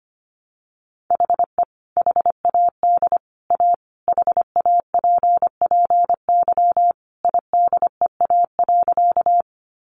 Morse code, heard as a keyed tone, about 25 wpm, 700 Hz. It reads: HE HAD A HAPPY IDEA.
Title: Sentences from Top 1000 Words 25wpm